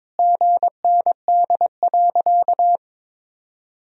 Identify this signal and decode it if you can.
Morse code, heard as a keyed tone, about 22 wpm, 700 Hz. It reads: GND.
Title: Sentences from Top 500 Words and Abbreviations 22wpm